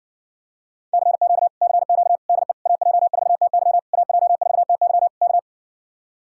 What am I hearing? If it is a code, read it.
Morse code, heard as a keyed tone, about 60 wpm, 700 Hz. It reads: CQ CQ DE AJ4TY AJ4TY K